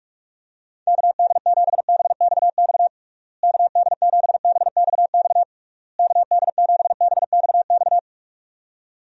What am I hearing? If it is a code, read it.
Morse code, heard as a keyed tone, about 46 wpm, 700 Hz. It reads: KD7BXX KD7BXX KD7BXX
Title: USA Call Signs - 6 Characters - Repeated 3x - 45wpm